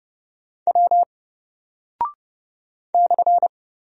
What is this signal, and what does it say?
Morse code, heard as a keyed tone, about 29 wpm, 700 Hz. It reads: W /
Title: Single Character (Letter:Number:Punctuation:Prosign) 30wpm